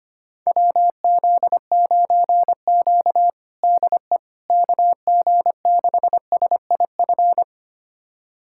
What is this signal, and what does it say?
Morse code, heard as a keyed tone, about 25 wpm, 700 Hz. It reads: WZ9Q DE KG6SIF
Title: USA Callsign DE Callsign 25wpm